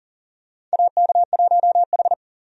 Morse code, heard as a keyed tone, about 40 wpm, 700 Hz. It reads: AK1H
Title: USA Call Signs - 4 Characters - 40wpm